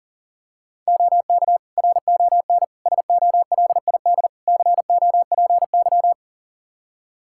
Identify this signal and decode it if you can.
Morse code, heard as a keyed tone, about 40 wpm, 700 Hz. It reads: OK RON SOLID COPY
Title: Practice QSO - Exchange 3 - More Info - 40wpm